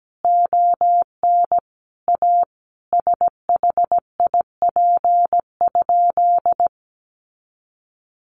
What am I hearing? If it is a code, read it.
Morse code, heard as a keyed tone, about 17 wpm, 700 Hz. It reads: ON A SHIP?
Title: Sentences from Top 400 Words 17wpm